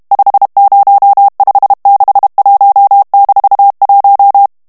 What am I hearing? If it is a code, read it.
Morse code, heard as a keyed tone, about 32 wpm, 800 Hz. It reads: 50561-1
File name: ALLEMAND texte clair 1 vitesse 30 mots minute